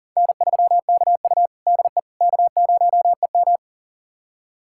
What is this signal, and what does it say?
Morse code, heard as a keyed tone, about 40 wpm, 700 Hz. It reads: N3KU DE K0EM
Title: USA Callsign DE Callsign - 4 Characters - 40wpm